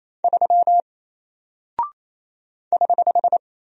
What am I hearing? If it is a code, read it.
Morse code, heard as a keyed tone, about 28 wpm, 700 Hz. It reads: 3 <HH>
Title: Single Character (Letter:Number:Punctuation:Prosign) Alt 28wpm